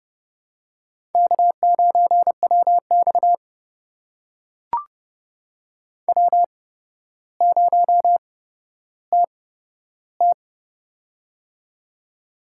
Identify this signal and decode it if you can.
Morse code, heard as a keyed tone, about 29 wpm, spaced as for 13 wpm, 700 Hz. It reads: K9WX W0TT
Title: USA Call Signs - 4 Characters - 8x Character Spacing - 30wpm